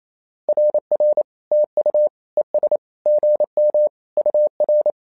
Morse code, heard as a keyed tone, about 28 wpm, 600 Hz. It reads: RR TU ES GM UR
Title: SOTA QSO Practice 28wpm